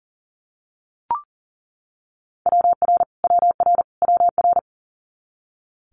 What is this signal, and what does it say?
Morse code, heard as a keyed tone, about 40 wpm, 700 Hz. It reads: WR WR WR
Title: Multi-letter Phonemes Repeated 3x 40wpm